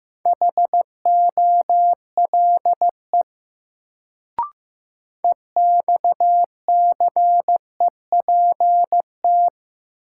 Morse code, heard as a keyed tone, about 15 wpm, 700 Hz. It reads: HOLE EXCEPT
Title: Sentences from Top 1600 Words - New Word Drill Alt - 15wpm